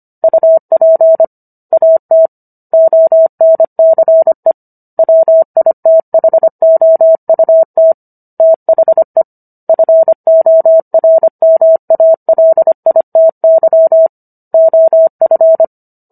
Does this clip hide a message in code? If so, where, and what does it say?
Yes: Morse code from 0.2 to 15.6 s, heard as a keyed tone, about 25 wpm, 650 Hz. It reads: UP AT ONCE WITHOUT THE FORMALITY OF